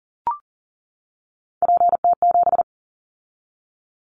Morse code, heard as a keyed tone, about 39 wpm, 700 Hz. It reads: PT7
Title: Callsign Prefixes for Top 100 DXCC - 40wpm